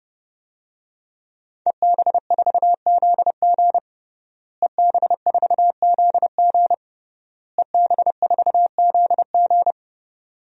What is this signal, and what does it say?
Morse code, heard as a keyed tone, about 32 wpm, 700 Hz. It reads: EB4ZG EB4ZG EB4ZG
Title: DXCC-100 Call Signs - 5 Characters - Repeated 3x - 30wpm